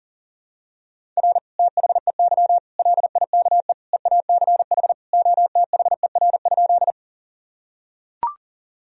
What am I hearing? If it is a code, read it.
Morse code, heard as a keyed tone, about 40 wpm, 700 Hz. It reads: R THEY LIKE EACH OTHER?
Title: Sentences from Top 200 Words and Abbreviations 40wpm